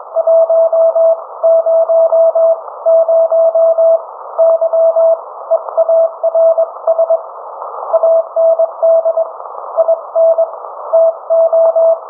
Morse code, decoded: 100 YEARS AND INTO